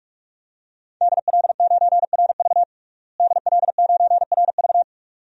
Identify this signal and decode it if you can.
Morse code, heard as a keyed tone, about 45 wpm, 700 Hz. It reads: DL9RV DL9RV
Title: DXCC-100 Call Signs - 5 Characters - Repeated 2x - 45wpm